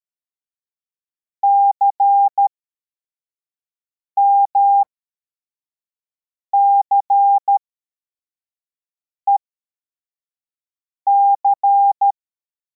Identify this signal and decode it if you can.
Morse code, heard as a keyed tone, about 13 wpm, spaced as for 4 wpm, 800 Hz. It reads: CMCEC
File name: formation / cw / Text34